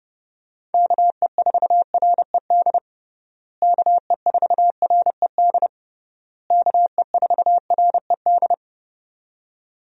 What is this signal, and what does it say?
Morse code, heard as a keyed tone, about 30 wpm, 700 Hz. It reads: KE4RED KE4RED KE4RED
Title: USA Call Signs - 6 Characters - Repeated 3x - 30wpm